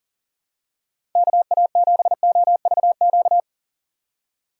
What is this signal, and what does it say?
Morse code, heard as a keyed tone, about 40 wpm, 700 Hz. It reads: KA7OVQ
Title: USA Call Signs - 6 Characters - 3x Character Spacing - 40wpm